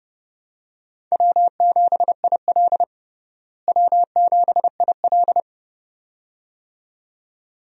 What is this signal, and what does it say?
Morse code, heard as a keyed tone, about 30 wpm, 700 Hz. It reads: W7IL W7IL
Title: USA Call Signs - 4 Characters - Repeated 2x - 30wpm